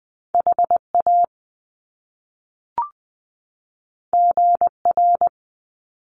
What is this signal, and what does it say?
Morse code, heard as a keyed tone, about 20 wpm, 700 Hz. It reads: HA GR
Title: Top Two Letter Combinations in Words 20wpm